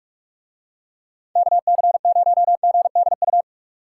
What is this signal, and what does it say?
Morse code, heard as a keyed tone, about 44 wpm, 700 Hz. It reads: KC0GDU